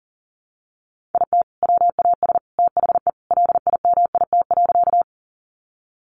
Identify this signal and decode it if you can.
Morse code, heard as a keyed tone, about 37 wpm, 700 Hz. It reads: IT WAS THE LIMIT.
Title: Sentences from Top 1500 Words 40wpm